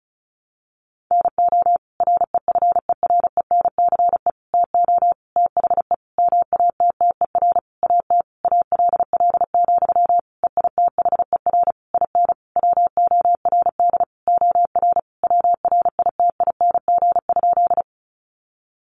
Morse code, heard as a keyed tone, about 34 wpm, 700 Hz. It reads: NO REFERENCE TO THE MATTER AT ALL, EITHER IN WORD OR WRITING?
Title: Sentences from Top 1200 Words 35wpm